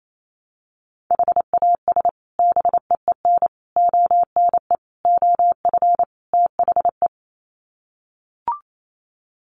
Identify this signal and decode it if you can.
Morse code, heard as a keyed tone, about 28 wpm, 700 Hz. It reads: HAS BEEN ONE OF THE
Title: Sets of 5 Words Encore 28wpm